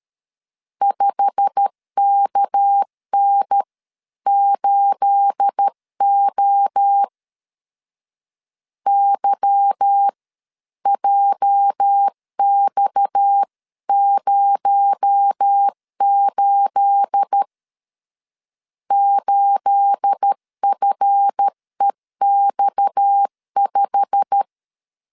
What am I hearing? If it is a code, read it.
Morse code, heard as a keyed tone, about 13 wpm, spaced as for 11 wpm, 800 Hz. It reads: 5KN8O YJX08 8FEX5
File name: LEZIONI / 9 28 12wpm 2351